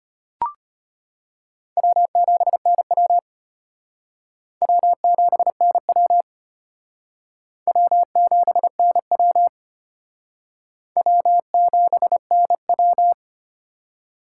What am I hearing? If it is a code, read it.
Morse code, heard as a keyed tone, about 34 wpm, 700 Hz. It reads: W7NW W7NW W7NW W7NW